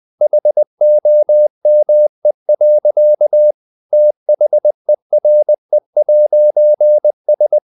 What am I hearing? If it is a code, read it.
Morse code, heard as a keyed tone, about 20 wpm, 600 Hz. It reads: HOME. THERE'S
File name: CW / Practice / EnglishSayings20